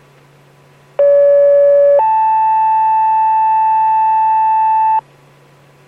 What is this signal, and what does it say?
Two-tone page, heard as a one-second tone, then a three-second tone, about 570 then 880 Hz, with noise. It is heard from 1.0 to 5.0 s.